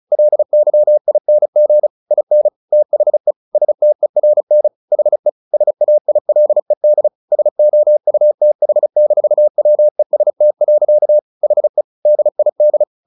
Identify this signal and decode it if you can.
Morse code, heard as a keyed tone, about 34 wpm, 600 Hz. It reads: LYING IN THE STERN HE SAILED SOUTH-WEST. HE DID
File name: the old man and the sea 2 35wpm